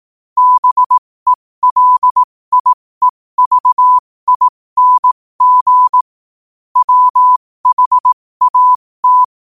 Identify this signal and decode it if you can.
Morse code, heard as a keyed tone, about 18 wpm, spaced as for 15 wpm, 1000 Hz. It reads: BELIEVING WHAT
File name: qotdfiles / 15wpm